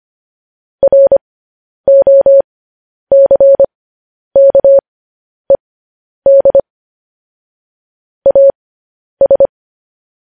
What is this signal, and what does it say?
Morse code, heard as a keyed tone, about 25 wpm, spaced as for 10 wpm, 550 Hz. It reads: ROCKED AS
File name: podcast / News headlines-1x-25x10